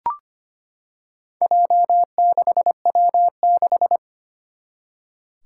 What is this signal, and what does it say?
Morse code, heard as a keyed tone, about 25 wpm, 700 Hz. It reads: J6W6
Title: Four Challenging Letter-Number Combinations 25wpm